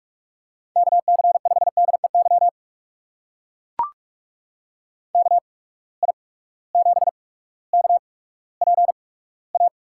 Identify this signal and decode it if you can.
Morse code, heard as a keyed tone, about 44 wpm, spaced as for 19 wpm, 700 Hz. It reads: KC5DEY KI7KPA
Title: USA Call Signs - 6 Characters - 8x Character Spacing - 45wpm